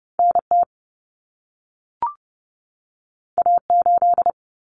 Morse code, heard as a keyed tone, about 30 wpm, 700 Hz. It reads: NT A8
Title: Common Cut Numbers - 2-Digit 30wpm